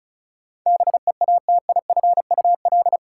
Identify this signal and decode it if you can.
Morse code, heard as a keyed tone, about 36 wpm, 700 Hz. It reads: BEATIFUL
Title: Sentences from Top 1300 Words - New Word Drill - 35wpm